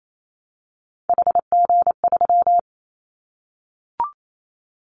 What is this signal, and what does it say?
Morse code, heard as a keyed tone, about 28 wpm, 700 Hz. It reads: HG3
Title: Callsign Prefixes for Top 50 DXCC - 28wpm